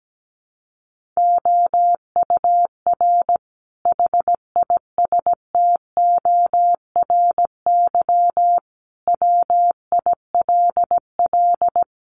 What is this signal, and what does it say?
Morse code, heard as a keyed tone, about 17 wpm, 700 Hz. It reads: OUR HISTORY WILL
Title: Sentences from Top 1500 Words 17wpm